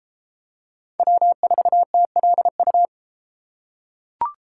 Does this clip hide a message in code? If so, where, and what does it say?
Yes: Morse code from 1.0 to 2.9 s, heard as a keyed tone, about 29 wpm, 700 Hz. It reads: W4TLU